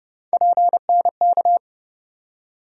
Morse code, heard as a keyed tone, about 30 wpm, 700 Hz. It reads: PNK
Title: USA Call Sign Postfixes - 3 Letters 30wpm